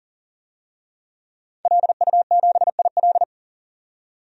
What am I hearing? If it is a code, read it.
Morse code, heard as a keyed tone, about 40 wpm, 700 Hz. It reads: LU7IL
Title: DXCC-100 Call Signs - 5 Characters - 40wpm